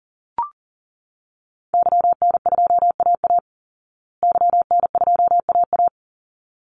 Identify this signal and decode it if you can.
Morse code, heard as a keyed tone, about 40 wpm, 700 Hz. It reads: YN2AA YN2AA
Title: Call Signs Repeated 3x - 40wpm